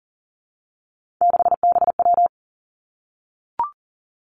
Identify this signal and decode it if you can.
Morse code, heard as a keyed tone, about 40 wpm, 700 Hz. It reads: 6BW
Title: Three Challenging Letter-Number Combinations 40wpm